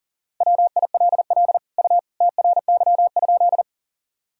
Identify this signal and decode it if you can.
Morse code, heard as a keyed tone, about 40 wpm, 700 Hz. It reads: WILL U TRY?
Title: Sentences from Top 200 Words and Abbreviations 40wpm